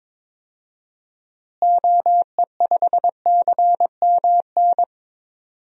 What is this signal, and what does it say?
Morse code, heard as a keyed tone, about 22 wpm, 700 Hz. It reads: OE5CMN
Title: DXCC-100 Call Signs - 6 Characters - Repeated 2x - 22wpm